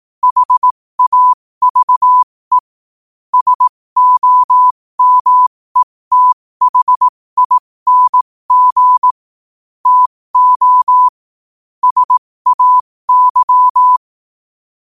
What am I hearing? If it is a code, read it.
Morse code, heard as a keyed tone, about 18 wpm, spaced as for 15 wpm, 1000 Hz. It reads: HAVE SOMETHING TO SAY
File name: qotdfiles / 15wpm